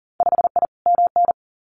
Morse code, heard as a keyed tone, about 42 wpm, 700 Hz. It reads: 5I MN
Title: Winter Field Day Exchanges Encore 40wpm